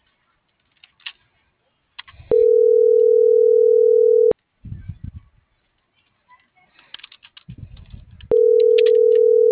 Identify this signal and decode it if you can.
Ringback tone (North America), 440 + 480 Hz.